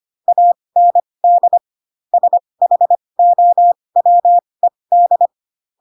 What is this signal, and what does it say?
Morse code, heard as a keyed tone, about 25 wpm, spaced as for 20 wpm, 700 Hz. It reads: AND SHOWED